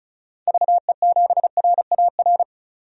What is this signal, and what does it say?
Morse code, heard as a keyed tone, about 36 wpm, 690 Hz. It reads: VE7RAR
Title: DXCC-100 Call Signs - 6 Characters - 35wpm